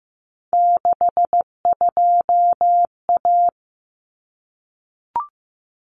Morse code, heard as a keyed tone, about 15 wpm, 700 Hz. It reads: 62A
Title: Common Cut Numbers - 3-Digit 15wpm